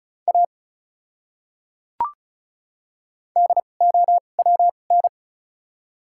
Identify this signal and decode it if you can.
Morse code, heard as a keyed tone, about 36 wpm, spaced as for 25 wpm, 700 Hz. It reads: A DOWN
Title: 2x Character Spacing - Top 200 Words - 35wpm